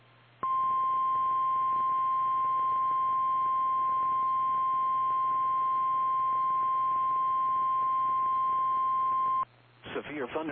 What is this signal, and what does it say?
Emergency Alert System (SAME) tones (the 1050 Hz attention tone), with speech. It is heard from 0.4 to 9.5 s.
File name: eas / KKPRKXPN / KKPRKXPN